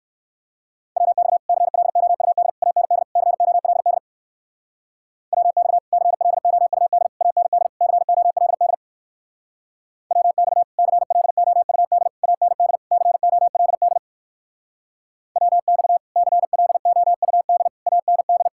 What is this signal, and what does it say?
Morse code, heard as a keyed tone, about 61 wpm, 700 Hz. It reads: WX CLOUD AND COLD WX CLOUD AND COLD WX CLOUD AND COLD WX CLOUD AND